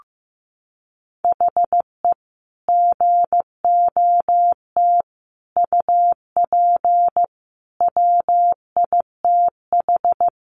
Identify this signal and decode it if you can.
Morse code, heard as a keyed tone, about 15 wpm, 700 Hz. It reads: HE GOT UP WITH
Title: Sentences from Top 1200 Words 15wpm